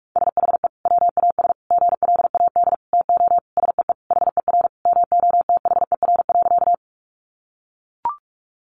Morse code, heard as a keyed tone, about 44 wpm, 700 Hz. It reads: SHE WAS GLAD TO SEE HER MOTHER.
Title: Sentences from Top 800 Words 45wpm